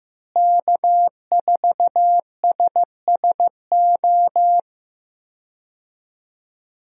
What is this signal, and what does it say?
Morse code, heard as a keyed tone, about 15 wpm, 700 Hz. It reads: K4SSO
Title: USA Call Signs - 5 Characters - Repeated 2x - 15wpm